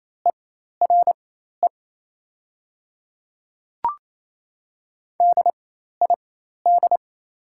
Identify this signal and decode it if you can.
Morse code, heard as a keyed tone, about 27 wpm, spaced as for 13 wpm, 700 Hz. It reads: ERE DID
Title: 4x Character Spacing - Top 500 Words - 28wpm